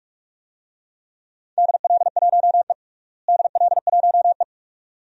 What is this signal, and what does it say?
Morse code, heard as a keyed tone, about 41 wpm, 700 Hz. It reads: DL1E DL1E
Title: DXCC-100 Call Signs - 4 Characters - Repeated 2x - 45wpm